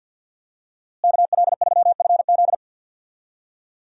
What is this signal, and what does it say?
Morse code, heard as a keyed tone, about 46 wpm, 700 Hz. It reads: KL3FZ
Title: USA Call Signs - 5 Characters - Repeated 3x - 50wpm